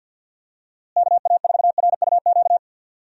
Morse code, heard as a keyed tone, about 47 wpm, 700 Hz. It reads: KA4RUQ
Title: USA Call Signs - 6 Characters - Repeated 2x - 50wpm